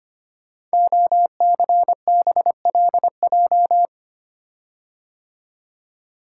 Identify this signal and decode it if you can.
Morse code, heard as a keyed tone, about 25 wpm, 700 Hz. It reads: OCBLJ
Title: Code Groups - 5 Letters - 25wpm